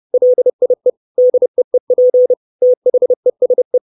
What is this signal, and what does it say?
Morse code, heard as a keyed tone, about 30 wpm, 500 Hz. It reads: LIE DEEP THESE
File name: 632 Words 30WPM